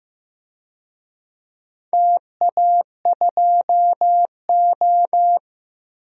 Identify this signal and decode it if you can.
Morse code, heard as a keyed tone, about 15 wpm, 700 Hz. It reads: TA2O
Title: DXCC-100 Call Signs - 4 Characters - Repeated 3x - 15wpm